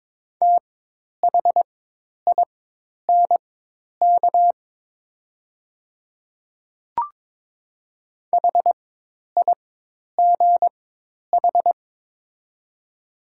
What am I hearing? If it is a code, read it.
Morse code, heard as a keyed tone, about 22 wpm, spaced as for 10 wpm, 700 Hz. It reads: THINK HIGH
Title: 4x Character Spacing - Top 200 Words - 22wpm